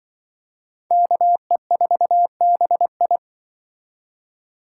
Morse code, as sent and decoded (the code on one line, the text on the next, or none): -.- . ....- -... ..
KE4BI